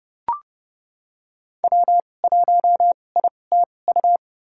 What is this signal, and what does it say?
Morse code, heard as a keyed tone, about 30 wpm, spaced as for 22 wpm, 700 Hz. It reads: W1ITU
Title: USA Call Signs - 2x Character Spacing - 30wpm